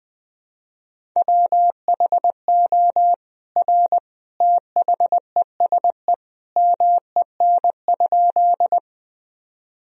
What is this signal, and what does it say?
Morse code, heard as a keyed tone, about 21 wpm, 700 Hz. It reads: WHO R THESE MEN?
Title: Sentences from Top 200 Words and Abbreviations 20wpm